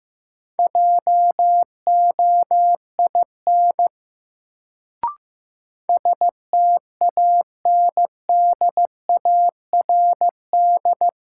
Morse code, heard as a keyed tone, about 16 wpm, 700 Hz. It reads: JOIN STANDARD
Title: Sentences from Top 1000 Words - Review Part 6 - New Word Drill - 15wpm